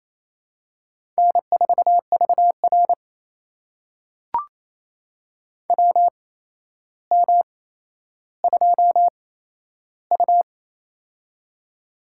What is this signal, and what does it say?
Morse code, heard as a keyed tone, about 28 wpm, spaced as for 12 wpm, 700 Hz. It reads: N4VR WM2U
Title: USA Call Signs - 4 Characters - 8x Character Spacing - 28wpm